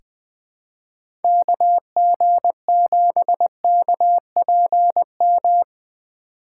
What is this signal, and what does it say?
Morse code, heard as a keyed tone, about 20 wpm, 700 Hz. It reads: KG7KPM
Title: USA Call Signs - 6 Characters - 2x Character Spacing - 20wpm